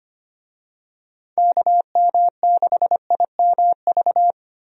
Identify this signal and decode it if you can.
Morse code, heard as a keyed tone, about 25 wpm, 700 Hz. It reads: KM6IMV
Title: USA Call Signs - 6 Characters - 25wpm